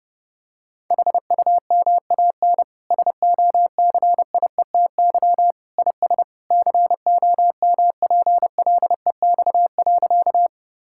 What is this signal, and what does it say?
Morse code, heard as a keyed tone, about 32 wpm, 700 Hz. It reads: HUMAN SOCIETY IS COMPLEX.
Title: Sentences from Top 1000 Words - Review Part 6 - 30wpm